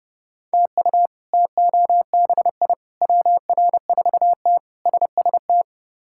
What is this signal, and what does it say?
Morse code, heard as a keyed tone, about 29 wpm, 700 Hz. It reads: TU TOBI WR4T SST
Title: K1USN SST QSO Practice - 30wpm